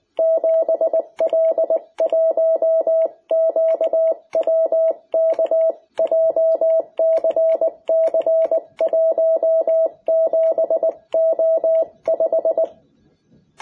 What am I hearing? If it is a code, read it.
Morse code, heard as a keyed tone, about 20 wpm, 660 Hz. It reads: 7L1QWK JCC17O5
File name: jh8cbh / 2021cw